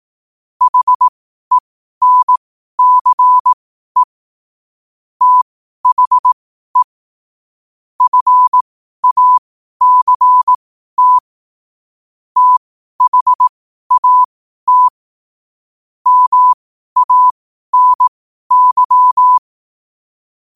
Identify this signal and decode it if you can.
Morse code, heard as a keyed tone, about 18 wpm, spaced as for 12 wpm, 1000 Hz. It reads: HENCE THE FACT THAT MANY
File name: qotdfiles / 12wpm